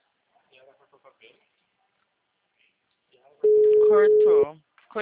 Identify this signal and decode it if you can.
Ringback tone (Europe, 425 Hz), with speech.
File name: out- -2019- - - .9684